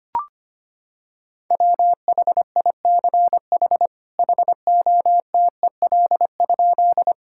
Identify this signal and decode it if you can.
Morse code, heard as a keyed tone, about 25 wpm, 700 Hz. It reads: WHICH HOTEL?
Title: Sentences from Top 1600 Words 25wpm